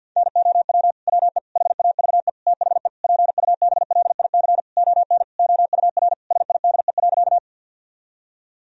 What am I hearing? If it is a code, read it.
Morse code, heard as a keyed tone, about 52 wpm, 700 Hz. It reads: NOW WE HAVE THE PUBLIC ON OUR SIDE.
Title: Sentences from Top 1000 Words - Review Part 3 - 50wpm